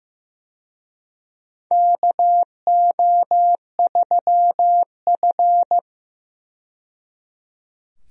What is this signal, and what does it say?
Morse code, heard as a keyed tone, about 15 wpm, 700 Hz. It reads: KO3F
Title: USA Call Signs - 4 Characters - 15wpm